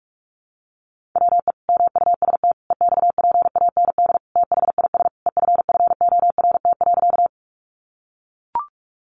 Morse code, heard as a keyed tone, about 44 wpm, 700 Hz. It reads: WE MUST EXPAND THIS EFFORT.
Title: Sentences from Top 1400 Words 45wpm